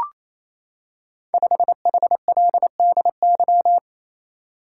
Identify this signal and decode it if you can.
Morse code, heard as a keyed tone, about 27 wpm, 700 Hz. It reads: 5HLDY
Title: Code Groups - 5 Letter:Numbers - 28wpm